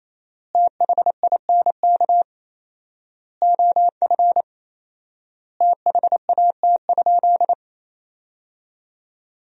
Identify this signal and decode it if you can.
Morse code, heard as a keyed tone, about 28 wpm, 700 Hz. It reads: THINK OF THAT?
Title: 4x Word-Spacing - Sentences from Top 200 Words - 28wpm